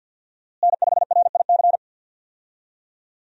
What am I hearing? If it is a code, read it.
Morse code, heard as a keyed tone, about 45 wpm, 700 Hz. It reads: N5RIC
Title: USA Call Signs - 5 Characters - Repeated 3x - 50wpm